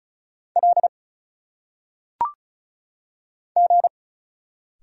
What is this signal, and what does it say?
Morse code, heard as a keyed tone, about 34 wpm, 700 Hz. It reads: L G
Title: Single Letters - Phonetic Pronunciation 35wpm